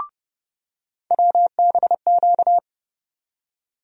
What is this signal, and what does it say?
Morse code, heard as a keyed tone, about 29 wpm, 700 Hz. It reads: WBQ
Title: USA Call Sign Postfixes - 3 Letters 30wpm